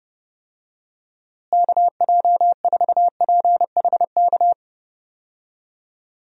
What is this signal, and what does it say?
Morse code, heard as a keyed tone, about 30 wpm, 700 Hz. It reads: KJ4PHK
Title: USA Call Signs - 6 Characters - Repeated 3x - 30wpm